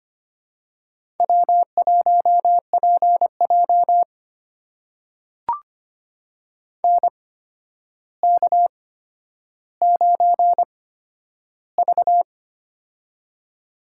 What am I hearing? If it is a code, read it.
Morse code, heard as a keyed tone, about 25 wpm, spaced as for 11 wpm, 700 Hz. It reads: W1PJ NK9V
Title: USA Call Signs - 4 Characters - 8x Character Spacing - 25wpm